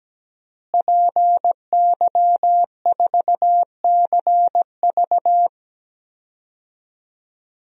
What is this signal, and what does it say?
Morse code, heard as a keyed tone, about 17 wpm, 700 Hz. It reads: PY4CV
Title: DXCC-100 Call Signs - 5 Characters - Repeated 2x - 17wpm